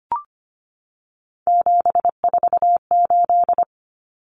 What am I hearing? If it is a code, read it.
Morse code, heard as a keyed tone, about 25 wpm, 700 Hz. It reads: 748
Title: Common Cut Numbers - 3-Digit 25wpm